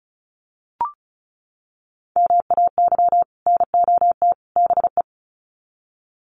Morse code, heard as a keyed tone, about 34 wpm, 700 Hz. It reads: MAY NOT BE
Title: Sets of 3 Words 35wpm